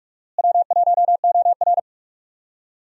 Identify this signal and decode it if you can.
Morse code, heard as a keyed tone, about 46 wpm, 700 Hz. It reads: W1OR
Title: USA Call Signs - 4 Characters - Repeated 3x - 45wpm